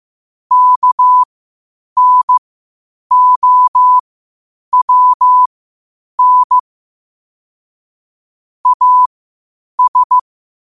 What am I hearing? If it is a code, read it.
Morse code, heard as a keyed tone, about 15 wpm, spaced as for 8 wpm, 1000 Hz. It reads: KNOWN AS
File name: qotdfiles / 8wpm